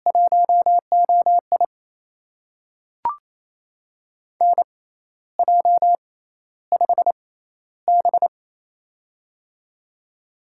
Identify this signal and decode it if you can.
Morse code, heard as a keyed tone, about 28 wpm, spaced as for 13 wpm, 700 Hz. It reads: 1OI NJ5B
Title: USA Call Signs - 4 Characters - 6x Character Spacing - 28wpm